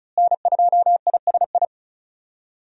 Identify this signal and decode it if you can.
Morse code, heard as a keyed tone, about 36 wpm, 700 Hz. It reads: N2ISI
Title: USA Call Signs - 5 Characters - Repeated 3x - 35wpm